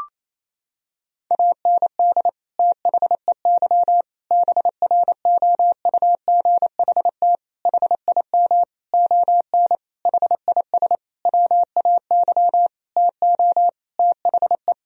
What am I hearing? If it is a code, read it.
Morse code, heard as a keyed tone, about 28 wpm, 700 Hz. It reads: AND THEY BROUGHT HIM ON HIS WAY TO THE
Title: Sentences from Top 1000 Words - Review Part 2 - 28wpm